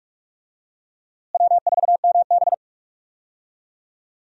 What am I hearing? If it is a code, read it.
Morse code, heard as a keyed tone, about 44 wpm, 700 Hz. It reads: W4MB
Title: USA Call Signs - 4 Characters - Repeated 2x - 45wpm